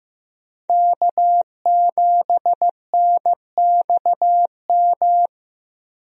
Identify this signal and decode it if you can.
Morse code, heard as a keyed tone, about 15 wpm, 700 Hz. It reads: K7NXM
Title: USA Call Signs - 5 Characters - Repeated 2x - 15wpm